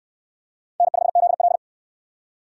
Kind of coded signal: Morse code

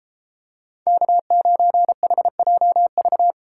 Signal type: Morse code

